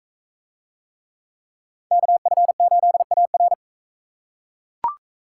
Morse code, heard as a keyed tone, about 41 wpm, 700 Hz. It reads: KF8AR